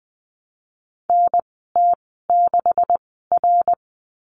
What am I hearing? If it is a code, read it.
Morse code, heard as a keyed tone, about 20 wpm, spaced as for 14 wpm, 700 Hz. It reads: NT6R
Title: 147 Lesson 24g - Letter y - Call Signs 20wpm